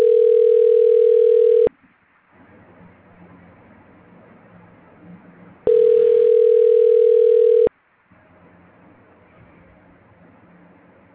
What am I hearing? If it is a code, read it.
Ringback tone (North America), 440 + 480 Hz.
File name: recordings / REb17dc0e538b60b952b9fdf5b8d37ae69